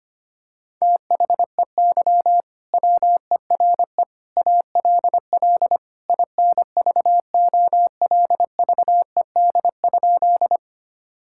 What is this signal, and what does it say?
Morse code, heard as a keyed tone, about 25 wpm, 700 Hz. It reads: THEY WERE ALL INVOLVED?
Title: Sentences from Top 1000 Words - Review Part 5 - 25wpm